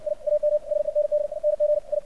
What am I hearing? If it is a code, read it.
Morse code, heard as a keyed tone, about 55 wpm, 600 Hz, with radio noise. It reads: I AM SENDNG I
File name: archived / CRSnet